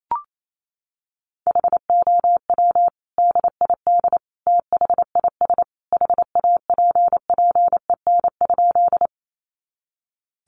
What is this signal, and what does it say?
Morse code, heard as a keyed tone, about 28 wpm, 700 Hz. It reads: HOW DID THIS HAPPEN?
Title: Sentences from Top 400 Words 28wpm